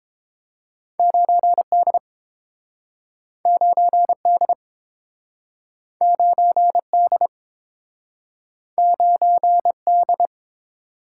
Morse code, heard as a keyed tone, about 30 wpm, 700 Hz. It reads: 9D 9D 9D 9D